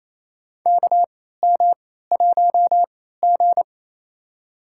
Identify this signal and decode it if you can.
Morse code, heard as a keyed tone, about 28 wpm, spaced as for 16 wpm, 700 Hz. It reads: KM1G
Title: USA Call Signs - 4 Characters - 3x Character Spacing - 28wpm